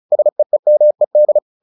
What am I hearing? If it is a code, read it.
Morse code, heard as a keyed tone, about 35 wpm, 600 Hz. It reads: SEEMED
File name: the old man and the sea 2 35wpm